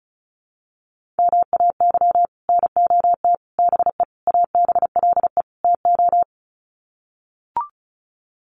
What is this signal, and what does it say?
Morse code, heard as a keyed tone, about 34 wpm, 700 Hz. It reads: MAY NOT BE ABLE TO
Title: Sets of 5 Words 35wpm